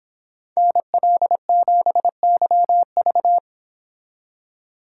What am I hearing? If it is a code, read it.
Morse code, heard as a keyed tone, about 30 wpm, 700 Hz. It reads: NL7YV